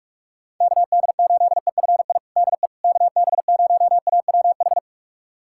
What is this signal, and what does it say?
Morse code, heard as a keyed tone, about 46 wpm, 700 Hz. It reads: KD8EFI DE KB0AWH
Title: USA Callsign DE Callsign - 6 Characters - 45wpm